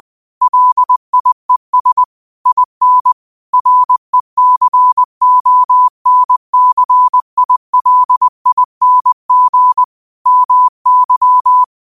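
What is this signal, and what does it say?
Morse code, heard as a keyed tone, about 20 wpm, 1000 Hz. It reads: LIES IN RECONCILING MY
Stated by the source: Morse code quote of the day at 20 wpm.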